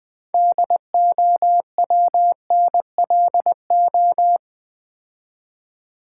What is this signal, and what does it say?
Morse code, heard as a keyed tone, about 20 wpm, 700 Hz. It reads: DOWNLO
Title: Build A Word - 2 Letters At A Time - 8-Letter Words - 20wpm